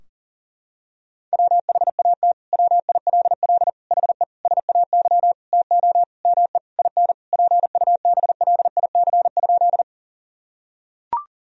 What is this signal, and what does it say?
Morse code, heard as a keyed tone, about 40 wpm, 700 Hz. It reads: WHAT WILL HE SAY TO ME IN PUBLIC?
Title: Sentences from Top 1000 Words - Review Part 3 - 40wpm